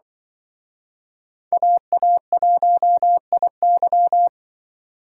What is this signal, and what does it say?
Morse code, heard as a keyed tone, about 24 wpm, 700 Hz. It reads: AA1IY
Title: Speed Racing - USA Call Signs 20wpm